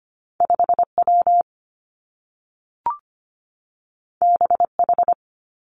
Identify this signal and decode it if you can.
Morse code, heard as a keyed tone, about 25 wpm, 700 Hz. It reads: 5 W B H
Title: Two Challenging Letter-Number Combinations 25wpm